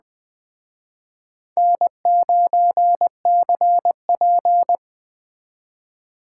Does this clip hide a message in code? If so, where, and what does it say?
Yes: Morse code from 1.6 to 4.8 s, heard as a keyed tone, about 20 wpm, 700 Hz. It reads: N9CP